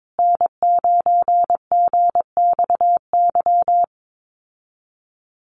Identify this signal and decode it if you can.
Morse code, heard as a keyed tone, about 22 wpm, 700 Hz. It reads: N9GXY
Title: USA Call Signs Repeated 3x 22wpm